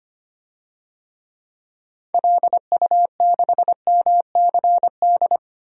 Morse code, heard as a keyed tone, about 25 wpm, 700 Hz. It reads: LU6MCD
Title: DXCC-100 Call Signs - 6 Characters - 25wpm